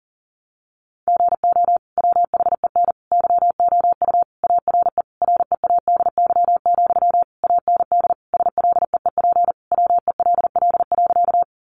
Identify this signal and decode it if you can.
Morse code, heard as a keyed tone, about 40 wpm, 700 Hz. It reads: GO WHEN YOU ARE READY, AND SLEEP WELL.
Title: Sentences from Top 500 Words 40wpm